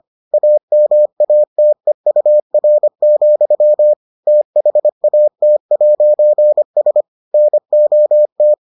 Morse code, heard as a keyed tone, about 25 wpm, 600 Hz. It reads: AMATEUR, THAT'S NOT